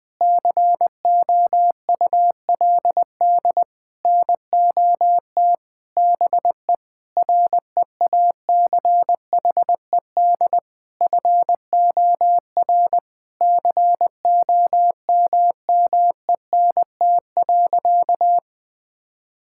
Morse code, heard as a keyed tone, about 20 wpm, 700 Hz. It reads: COULD NOT BE REACHED FOR COMMENT.
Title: Sentences with 6-Word Phrases - 20wpm